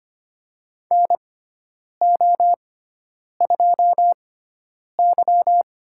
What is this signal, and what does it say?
Morse code, heard as a keyed tone, about 25 wpm, spaced as for 9 wpm, 700 Hz. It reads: NO2Y
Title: USA Call Signs - 6x Character Spacing - 25wpm